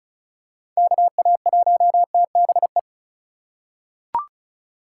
Morse code, heard as a keyed tone, about 36 wpm, 700 Hz. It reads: KA1TBE
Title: USA Call Signs - 6 Characters - 3x Character Spacing - 35wpm